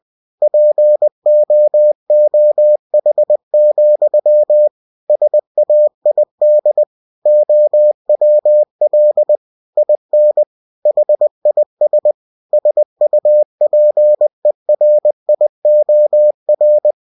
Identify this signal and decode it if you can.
Morse code, heard as a keyed tone, about 20 wpm, 600 Hz. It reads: POOH, SAID OWL IN HIS SUPERIOR
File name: cw-books / winnie-the-pooh / winnie-20wpm-page-0082